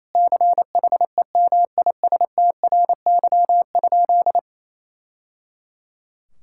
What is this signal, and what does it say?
Morse code, heard as a keyed tone, about 28 wpm, 700 Hz. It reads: CHEMISTRY?
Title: Sentences from Top 1000 Words - Review Part 6 - 28wpm